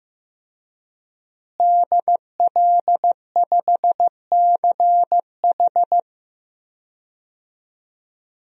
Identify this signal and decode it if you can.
Morse code, heard as a keyed tone, about 15 wpm, 700 Hz. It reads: DL5CH
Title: DXCC-100 Call Signs - 5 Characters - Repeated 3x - 15wpm